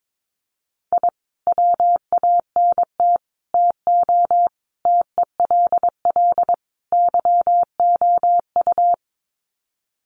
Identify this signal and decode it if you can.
Morse code, heard as a keyed tone, about 22 wpm, 700 Hz. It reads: I WANT TO TELL YOU
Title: Sets of 5 Words Encore 22wpm